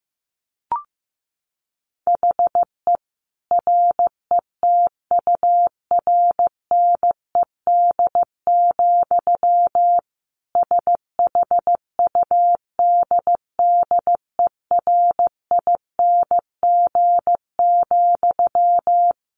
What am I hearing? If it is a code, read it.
Morse code, heard as a keyed tone, about 15 wpm, 700 Hz. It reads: HE RETURNED, SHUDDERING,
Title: A Sound Of Thunder 15wpm